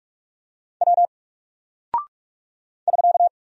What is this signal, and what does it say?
Morse code, heard as a keyed tone, about 44 wpm, 700 Hz. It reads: W <SK>
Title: Single Character (Letter:Number:Punctuation:Prosign) 45wpm